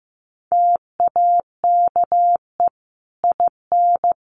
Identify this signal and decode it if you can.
Morse code, heard as a keyed tone, about 15 wpm, 700 Hz. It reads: TAKE IN
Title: Sentences from Top 1400 Words 15wpm